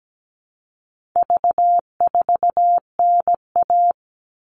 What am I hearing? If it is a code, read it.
Morse code, heard as a keyed tone, about 17 wpm, 700 Hz. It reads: V4NA
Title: All Cut Numbers - 4-Digit 17wpm